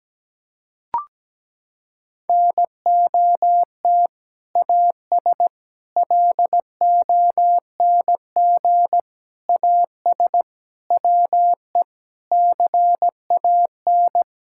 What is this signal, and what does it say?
Morse code, heard as a keyed tone, about 17 wpm, 700 Hz. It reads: NOT AS LONG AS WE CAN
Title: Sentences from Top 1000 Words - Review Part 6 - 17wpm